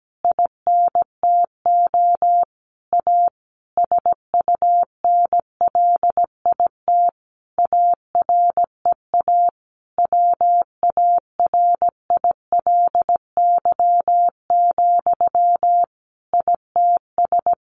Morse code, heard as a keyed tone, about 17 wpm, 700 Hz. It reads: INTO A SUNLIT AREA WARILY, ITS
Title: A Sound Of Thunder 17wpm